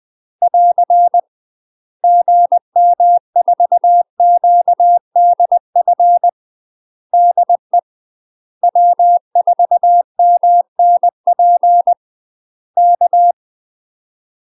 Morse code, heard as a keyed tone, about 20 wpm, 700 Hz. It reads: <AR> GM4QDF DE W4MNP K
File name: UK MASTER QSOs TRAINING / 20WPM / UK MASTER QSOs 0058